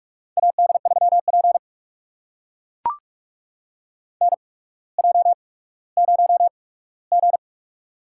Morse code, heard as a keyed tone, about 44 wpm, spaced as for 19 wpm, 700 Hz. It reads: AD3P NJ0G
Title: USA Call Signs - 4 Characters - 8x Character Spacing - 45wpm